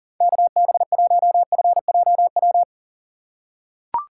Morse code, heard as a keyed tone, about 38 wpm, 700 Hz. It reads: KB1FJW
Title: USA Call Signs - 6 Characters - 2x Character Spacing - 40wpm